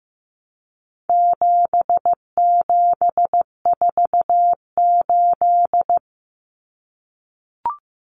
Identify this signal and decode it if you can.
Morse code, heard as a keyed tone, about 15 wpm, 700 Hz. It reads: 7748
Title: Common Cut Numbers - 4-Digit 15wpm